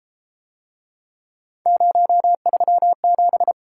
Morse code, heard as a keyed tone, about 33 wpm, 700 Hz. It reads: 037